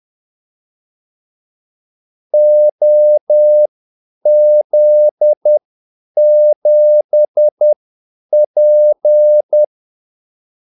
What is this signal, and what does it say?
Morse code, heard as a keyed tone, about 10 wpm, spaced as for 8 wpm, 600 Hz. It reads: OZ7P
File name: Callsigns-10-8wpm